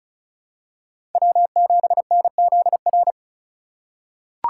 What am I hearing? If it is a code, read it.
Morse code, heard as a keyed tone, about 34 wpm, 700 Hz. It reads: W7NZR
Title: USA Call Signs - 5 Characters - 2x Character Spacing - 35wpm